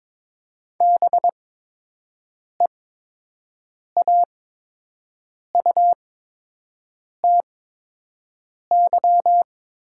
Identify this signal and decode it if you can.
Morse code, heard as a keyed tone, about 22 wpm, spaced as for 6 wpm, 700 Hz. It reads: BEAUTY
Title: 8x Character Spacing - Top 500 Words - 22wpm